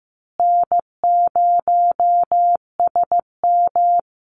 Morse code, heard as a keyed tone, about 15 wpm, 700 Hz. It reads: N0SM
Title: Easy Call Signs Repeated 3x - 15wpm